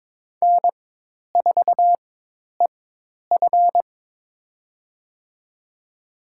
Morse code, heard as a keyed tone, about 22 wpm, spaced as for 10 wpm, 700 Hz. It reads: N4EF
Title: USA Call Signs - 4x Character Spacing - 22wpm